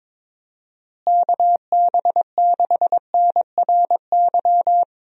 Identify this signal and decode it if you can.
Morse code, heard as a keyed tone, about 22 wpm, 700 Hz. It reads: KB6NRY
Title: USA Call Signs - 6 Characters - Repeated 3x - 22wpm